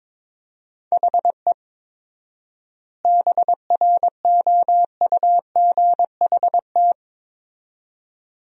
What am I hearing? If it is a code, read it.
Morse code, heard as a keyed tone, about 22 wpm, 700 Hz. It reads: HE BROUGHT
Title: 4x Word-Spacing - Sentences from Top 500 Words - 22wpm